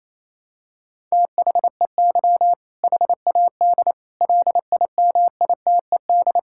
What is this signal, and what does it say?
Morse code, heard as a keyed tone, about 28 wpm, 700 Hz. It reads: THEY HAD LIMITED
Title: Sentences from Top 1000 Words - Review Part 6 - 28wpm